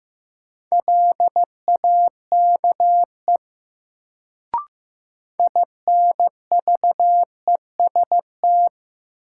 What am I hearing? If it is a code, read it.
Morse code, heard as a keyed tone, about 15 wpm, 700 Hz. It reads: LAKE INVEST
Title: Sentences from Top 1100 Words - New Word Drill Alt - 15wpm